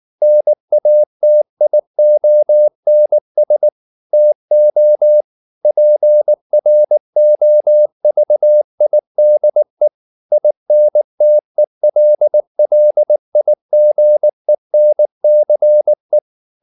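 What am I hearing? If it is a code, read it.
Morse code, heard as a keyed tone, about 19 wpm, 600 Hz. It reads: NATIONS TO PROVIDE INTELLIGENCE
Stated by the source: Morse code at 19 wpm from CW med Gnister, reading Ken Follett's novel Never in English.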